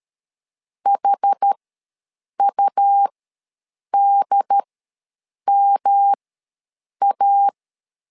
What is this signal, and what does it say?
Morse code, heard as a keyed tone, about 13 wpm, spaced as for 7 wpm, 800 Hz. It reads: HUDMA
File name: LEZIONI / 3 MWRO / 3 M ws 12wpm 1745